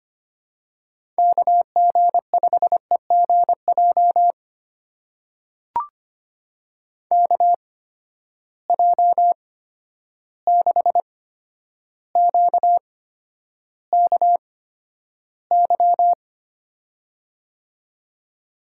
Morse code, heard as a keyed tone, about 25 wpm, spaced as for 11 wpm, 700 Hz. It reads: KG5EGJ KJ6QKY
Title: USA Call Signs - 6 Characters - 8x Character Spacing - 25wpm